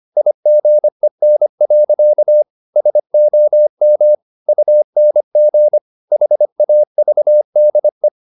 Morse code, heard as a keyed tone, about 25 wpm, 600 Hz. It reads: IGEN. SOM UNG HAVDE
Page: Ferie 25 wpm